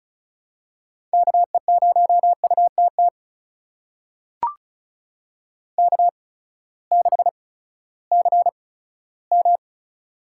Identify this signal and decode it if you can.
Morse code, heard as a keyed tone, about 36 wpm, spaced as for 18 wpm, 700 Hz. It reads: KE0UTT K6CM
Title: USA Call Signs - 8x Character Spacing - 35wpm